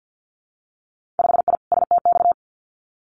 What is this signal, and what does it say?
Morse code, heard as a keyed tone, about 50 wpm, 700 Hz. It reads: 5I STX
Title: Winter Field Day Exchanges Encore 50wpm